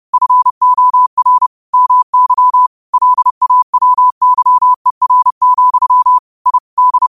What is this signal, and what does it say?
Morse code, heard as a keyed tone, about 30 wpm, 1000 Hz. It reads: FOR MY LAWYER, I D
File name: qotdfiles / 30wpm